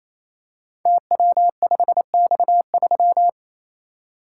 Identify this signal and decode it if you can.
Morse code, heard as a keyed tone, about 27 wpm, 700 Hz. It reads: TW5X3
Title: Code Groups - 5 Letter:Numbers - 28wpm